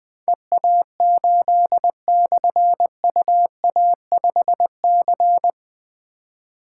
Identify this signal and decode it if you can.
Morse code, heard as a keyed tone, about 20 wpm, 700 Hz. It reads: EA8/UA5C
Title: Difficult Call Signs 20wpm